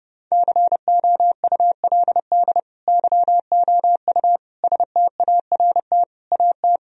Morse code, heard as a keyed tone, about 32 wpm, 700 Hz. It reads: COULD YOU START AT
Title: Sentences from Top 1000 Words - Review Part 1 - 30wpm